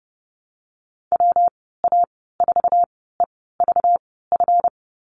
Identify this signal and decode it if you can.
Morse code, heard as a keyed tone, about 30 wpm, spaced as for 17 wpm, 700 Hz. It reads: WA4EVF
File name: Lesson 37g - Letter x - Call Signs 30wpm